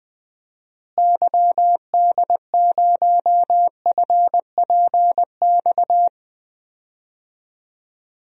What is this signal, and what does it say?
Morse code, heard as a keyed tone, about 21 wpm, 700 Hz. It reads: YD0FPX
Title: DXCC-100 Call Signs - 6 Characters - Repeated 3x - 20wpm